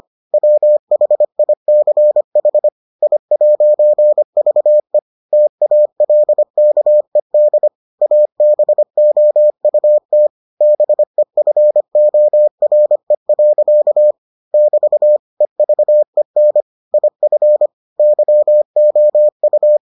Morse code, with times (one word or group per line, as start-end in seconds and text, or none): 0.3-2.7 WHICH
3.0-5.0 I'VE
5.3-7.7 TALKED
8.0-10.3 ABOUT
10.6-14.1 BEFORE.
14.5-15.2 <BT>
15.4-16.6 EVEN
16.9-17.7 IF
18.0-19.8 YOU